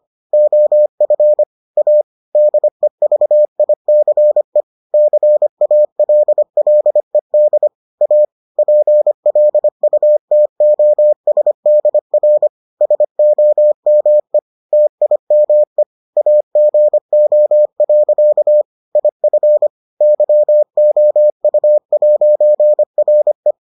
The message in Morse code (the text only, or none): OF A DEVICE CALLED A PLUTOSDR SOME TIME AGO. IF YOU'RE